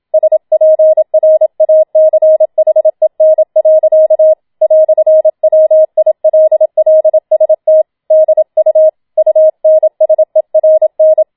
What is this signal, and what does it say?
Morse code, heard as a keyed tone, about 27 wpm, 630 Hz. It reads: SPRACHEN. "WILLST DU UNSERN